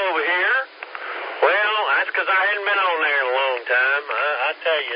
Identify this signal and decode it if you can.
Emergency Alert System (SAME) tones, heard as harsh buzzing bursts, with speech. It is heard from 0.3 to 4.0 s.